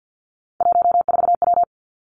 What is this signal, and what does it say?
Morse code, heard as a keyed tone, about 50 wpm, 700 Hz. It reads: 14W
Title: Three Challenging Letter-Number Combinations 50wpm